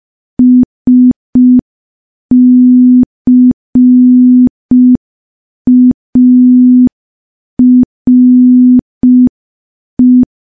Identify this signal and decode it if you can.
Morse code, heard as a keyed tone, about 5 wpm, 250 Hz. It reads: SCARE
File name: morse / morse-training-05WPM